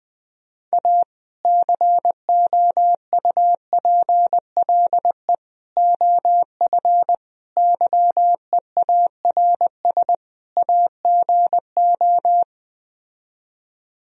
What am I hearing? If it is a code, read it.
Morse code, heard as a keyed tone, about 20 wpm, 700 Hz. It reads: A COUPLE OF YEARS AGO